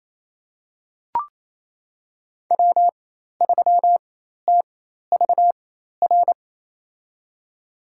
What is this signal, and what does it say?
Morse code, heard as a keyed tone, about 28 wpm, spaced as for 13 wpm, 700 Hz. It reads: W3TVR
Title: USA Call Signs - 5 Characters - 4x Character Spacing - 28wpm